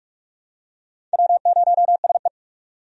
Morse code, heard as a keyed tone, about 46 wpm, 700 Hz. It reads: W0SE
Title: USA Call Signs - 4 Characters - 45wpm